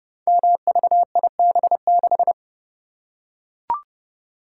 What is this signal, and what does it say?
Morse code, heard as a keyed tone, about 29 wpm, 700 Hz. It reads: MVIB6
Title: Code Groups - 5 Letter:Numbers - 30wpm